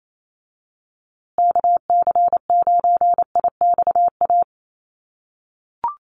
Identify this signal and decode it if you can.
Morse code, heard as a keyed tone, about 28 wpm, 700 Hz. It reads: KC9IXA
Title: USA Call Signs Repeated 3x 28wpm